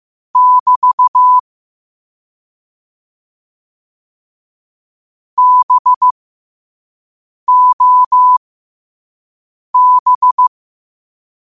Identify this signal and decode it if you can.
Morse code, heard as a keyed tone, about 15 wpm, spaced as for 5 wpm, 1000 Hz. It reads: <BT> BOB
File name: qotdfiles / 5wpm